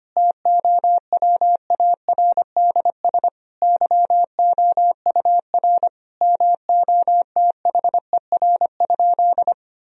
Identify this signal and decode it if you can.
Morse code, heard as a keyed tone, about 25 wpm, 700 Hz. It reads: TOWARDS YOUR MOTHER?
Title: Sentences from Top 1000 Words - Review Part 2 - 25wpm